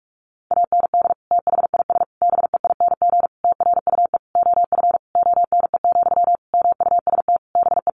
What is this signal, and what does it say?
Morse code, heard as a keyed tone, about 44 wpm, 700 Hz. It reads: AND THIS BEING TRUE OF ONE, MUST BE
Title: Sentences from Top 1400 Words 45wpm